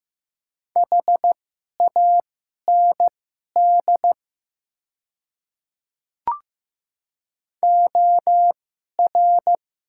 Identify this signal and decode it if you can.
Morse code, heard as a keyed tone, about 15 wpm, spaced as for 11 wpm, 700 Hz. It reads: HAND OR
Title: 2x Character Spacing - Top 200 Words - 15wpm